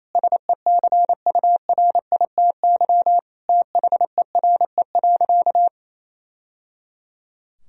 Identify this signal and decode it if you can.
Morse code, heard as a keyed tone, about 28 wpm, 700 Hz. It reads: SECURITY THERE.
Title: Sentences from Top 1000 Words - Review Part 4 - 28wpm